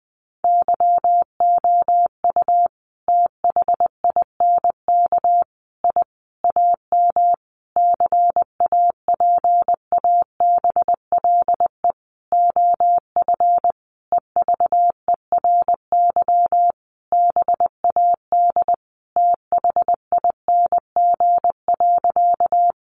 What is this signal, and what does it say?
Morse code, heard as a keyed tone, about 20 wpm, 700 Hz. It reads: YOU THINK I AM CAPABLE OF EVERY BAD THING.
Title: Sentences from Top 1200 Words 20wpm